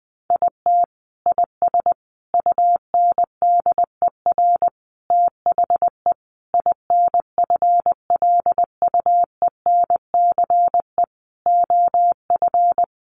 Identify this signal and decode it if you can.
Morse code, heard as a keyed tone, about 20 wpm, 700 Hz. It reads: IT IS UNDER THE INFLUENCE OF
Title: Sentences from Top 1400 Words 20wpm